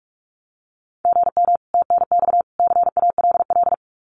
Morse code, heard as a keyed tone, about 44 wpm, 700 Hz. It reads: GM TNX CALL
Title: Common QSO Phrases 45wpm